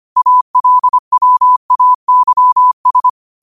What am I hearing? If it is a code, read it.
Morse code, heard as a keyed tone, about 25 wpm, 1000 Hz. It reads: ALWAYS
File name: qotdfiles / 25wpm